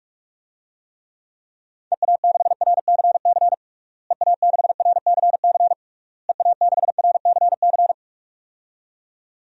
Morse code, heard as a keyed tone, about 46 wpm, 700 Hz. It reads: EA6RCC EA6RCC EA6RCC
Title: DXCC-100 Call Signs - 6 Characters - Repeated 3x - 45wpm